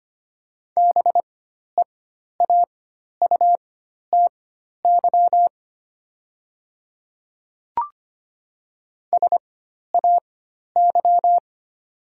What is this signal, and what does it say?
Morse code, heard as a keyed tone, about 25 wpm, spaced as for 12 wpm, 700 Hz. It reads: BEAUTY SAY
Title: 4x Character Spacing - Top 500 Words - 25wpm